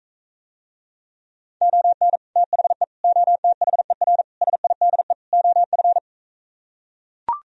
Morse code, heard as a keyed tone, about 45 wpm, 700 Hz. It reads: ON THE OTHER SIDE OF